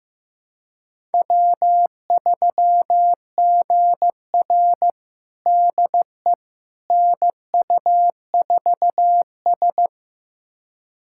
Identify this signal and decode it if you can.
Morse code, heard as a keyed tone, about 15 wpm, 700 Hz. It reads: W3GR DE NU4S
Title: USA Callsign DE Callsign - 4 Characters - 15wpm